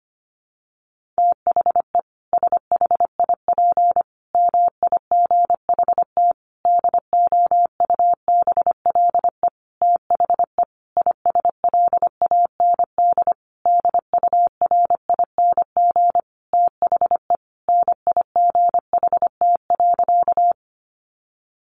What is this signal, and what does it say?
Morse code, heard as a keyed tone, about 25 wpm, 700 Hz. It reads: THE SHIP MIGHT DOUBLE THE ISLAND DURING THE NIGHT.
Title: Sentences from Top 1400 Words 25wpm